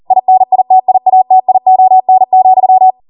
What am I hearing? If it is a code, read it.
Morse code, heard as a keyed tone, about 40 wpm, 750 Hz. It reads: INITIATION,